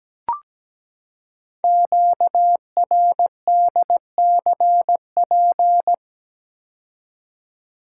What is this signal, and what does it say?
Morse code, heard as a keyed tone, about 17 wpm, 700 Hz. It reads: QRDCP
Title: Code Groups - 5 Letter:Numbers - 17wpm